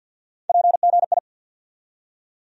Morse code, heard as a keyed tone, about 52 wpm, 700 Hz. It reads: PGI
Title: USA Call Sign Postfixes - 3 Letters 50wpm